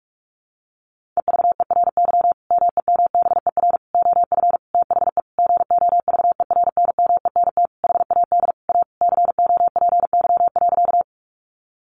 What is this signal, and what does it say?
Morse code, heard as a keyed tone, about 44 wpm, 700 Hz. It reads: EVERY MEMBER OF THE GOVERNMENT HAD A COPY.
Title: Sentences from Top 1400 Words 45wpm